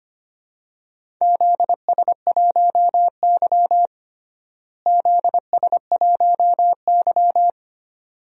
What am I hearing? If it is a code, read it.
Morse code, heard as a keyed tone, about 25 wpm, 700 Hz. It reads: ZS1Y ZS1Y
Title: DXCC-100 Call Signs - 4 Characters - Repeated 2x - 25wpm